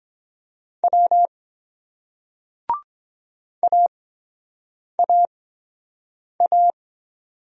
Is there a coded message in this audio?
Yes: Morse code from 0.8 to 6.7 s, heard as a keyed tone, about 23 wpm, spaced as for 7 wpm, 700 Hz. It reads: W AAA